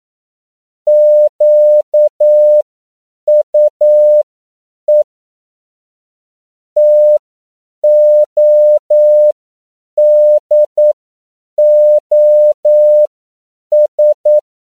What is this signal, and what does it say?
Morse code, heard as a keyed tone, about 9 wpm, spaced as for 7 wpm, 600 Hz. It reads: QUE TODOS